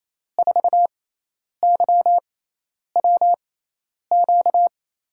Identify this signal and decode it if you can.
Morse code, heard as a keyed tone, about 28 wpm, spaced as for 10 wpm, 700 Hz. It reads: 4YWQ
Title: USA Call Signs - 6 Characters - 6x Character Spacing - 28wpm